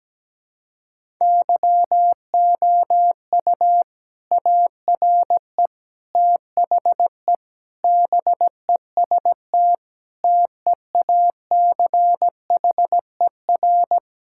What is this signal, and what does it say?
Morse code, heard as a keyed tone, about 17 wpm, 700 Hz. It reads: YOU ARE THE BEST TEACHER
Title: Sentences from Top 1000 Words - Review Part 6 - 17wpm